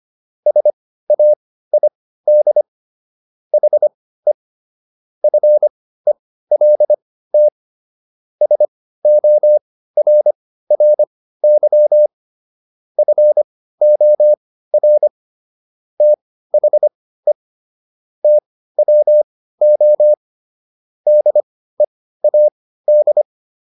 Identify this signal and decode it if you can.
Morse code, heard as a keyed tone, about 25 wpm, spaced as for 15 wpm, 600 Hz. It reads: SAID HE FELT SORRY FOR THE TWO DEAD